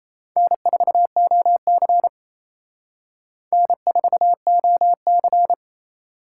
Morse code, heard as a keyed tone, about 34 wpm, 700 Hz. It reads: N4OC N4OC
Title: Speed Racing - USA 1x2 Call Signs 28wpm